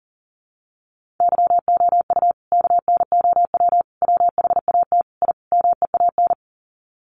Morse code, heard as a keyed tone, about 40 wpm, 700 Hz. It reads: YOU KNOW WHAT I MEAN
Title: Sets of 5 Words 40wpm